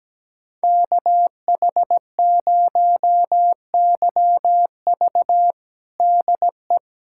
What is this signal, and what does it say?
Morse code, heard as a keyed tone, about 17 wpm, 700 Hz. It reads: KH0YV DE
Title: USA Callsign DE Callsign - 5 Characters - 17wpm